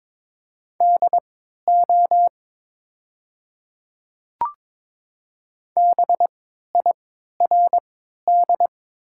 Morse code, heard as a keyed tone, about 22 wpm, spaced as for 13 wpm, 700 Hz. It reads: DO BIRD
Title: 3x Character Spacing - Top 500 Words - 22wpm